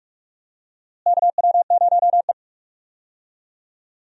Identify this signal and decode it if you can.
Morse code, heard as a keyed tone, about 46 wpm, 700 Hz. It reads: KW0E
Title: USA Call Signs - 4 Characters - Repeated 3x - 45wpm